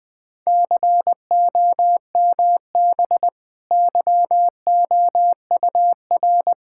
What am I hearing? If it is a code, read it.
Morse code, heard as a keyed tone, about 20 wpm, 700 Hz. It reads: COMB YOUR
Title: Easy Sentences with Hints - 20wpm